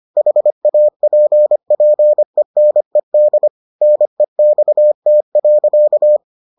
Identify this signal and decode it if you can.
Morse code, heard as a keyed tone, about 25 wpm, 600 Hz. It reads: HAPPENED NEXT.